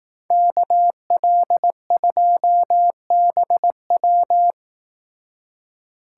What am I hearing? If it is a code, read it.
Morse code, heard as a keyed tone, about 18 wpm, 700 Hz. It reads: KL2BW
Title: Speed Racing - USA 2x2 Call Signs 15wpm